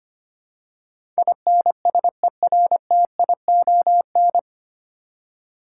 Morse code, heard as a keyed tone, about 25 wpm, 700 Hz. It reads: INSERTION
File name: Nine Letter Words - 25wpm